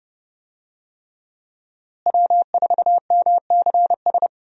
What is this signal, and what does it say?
Morse code, heard as a keyed tone, about 31 wpm, 700 Hz. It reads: W4MCS
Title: USA Call Signs - 5 Characters - Repeated 3x - 30wpm